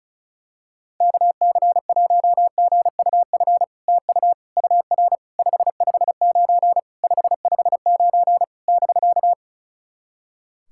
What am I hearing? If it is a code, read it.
Morse code, heard as a keyed tone, about 37 wpm, 700 Hz. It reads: KC1GUF TU UR 559 559 <BK>
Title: SOTA QSO Practice 35wpm